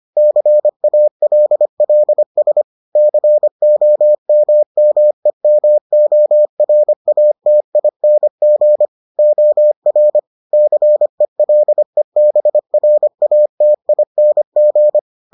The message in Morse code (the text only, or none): CALLS COMMEMORATING OR CELEBRATING